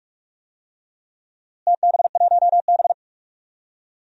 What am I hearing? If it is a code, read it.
Morse code, heard as a keyed tone, about 42 wpm, 700 Hz. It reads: TB1B
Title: DXCC-100 Call Signs - 4 Characters - Repeated 2x - 45wpm